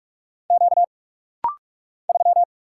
Morse code, heard as a keyed tone, about 44 wpm, 700 Hz. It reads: Q 3
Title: Single Letter-Number - Rapid-Fire 45wpm